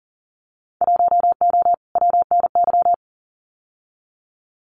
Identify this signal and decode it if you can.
Morse code, heard as a keyed tone, about 40 wpm, 700 Hz. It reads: 1O WNY
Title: Winter Field Day Exchanges Encore 40wpm